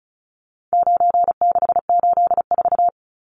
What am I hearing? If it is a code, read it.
Morse code, heard as a keyed tone, about 34 wpm, 700 Hz. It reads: 9684
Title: Numbers - 4-Digit 35wpm